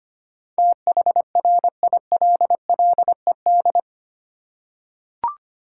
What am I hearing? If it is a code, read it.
Morse code, heard as a keyed tone, about 25 wpm, 700 Hz. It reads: THRILLED
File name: Eight Letter Words - 25wpm